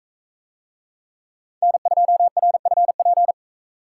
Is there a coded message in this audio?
Yes: Morse code from 1.6 to 3.3 s, heard as a keyed tone, about 39 wpm, 700 Hz. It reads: N2RFP